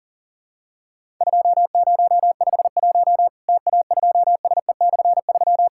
Morse code, heard as a keyed tone, about 40 wpm, 700 Hz. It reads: 2051 TA2SE/3
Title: Worked All Europe DX Contest - QTC Exchange - 40wpm